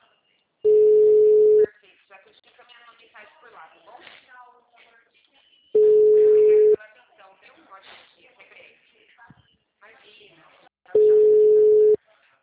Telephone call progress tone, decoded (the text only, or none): ringback tone